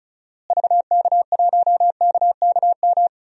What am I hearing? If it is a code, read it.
Morse code, heard as a keyed tone, about 34 wpm, 700 Hz. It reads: VK1KKM
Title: DXCC-100 Call Signs - 6 Characters - Repeated 3x - 35wpm